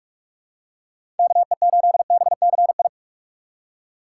Morse code, heard as a keyed tone, about 44 wpm, 700 Hz. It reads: KE8BCI
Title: USA Call Signs - 6 Characters - 6x Character Spacing - 45wpm